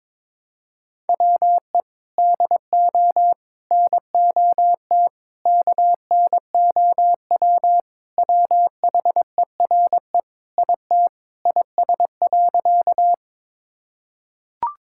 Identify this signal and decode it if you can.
Morse code, heard as a keyed tone, about 22 wpm, 700 Hz. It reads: WE DO NOT KNOW WHERE IT IS.
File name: practice / 22wpm / Sentences with 4-Word Phrases - 22wpm